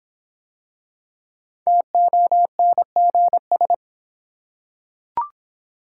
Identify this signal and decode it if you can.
Morse code, heard as a keyed tone, about 26 wpm, 700 Hz. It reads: TONGS